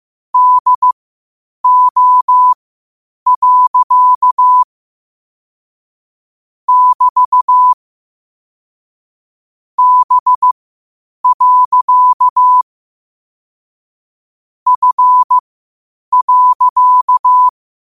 Morse code, heard as a keyed tone, about 15 wpm, spaced as for 8 wpm, 1000 Hz. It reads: DO. <BT> B. F.
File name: qotdfiles / 8wpm